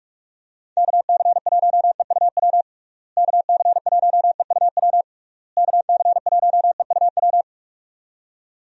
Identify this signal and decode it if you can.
Morse code, heard as a keyed tone, about 46 wpm, 700 Hz. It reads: KC1EUW KC1EUW KC1EUW
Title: USA Call Signs - 6 Characters - Repeated 3x - 45wpm